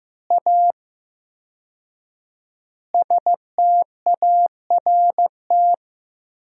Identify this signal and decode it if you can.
Morse code, heard as a keyed tone, about 15 wpm, 700 Hz. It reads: A START
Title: 4x Word-Spacing - Sentences from Top 300 Words - 15wpm